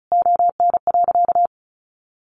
Morse code, heard as a keyed tone, about 34 wpm, 700 Hz. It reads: ON.
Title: Sentences from Top 1100 Words 35wpm